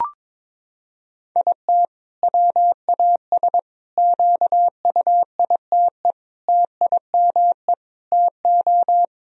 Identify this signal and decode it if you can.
Morse code, heard as a keyed tone, about 22 wpm, 700 Hz. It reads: IT WAS QUITE TIME TO
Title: Sentences from Top 1000 Words - Review Part 1 - 22wpm